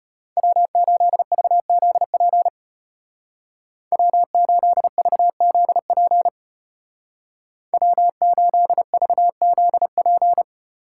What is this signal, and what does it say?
Morse code, heard as a keyed tone, about 34 wpm, 700 Hz. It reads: W8VZP W8VZP W8VZP